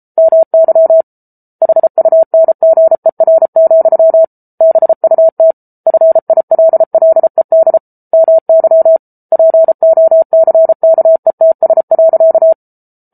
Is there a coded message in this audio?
Yes: Morse code from 0.2 to 12.5 s, heard as a keyed tone, about 34 wpm, 650 Hz. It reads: MY HUNGER, BUT FILLED MY POCKETS.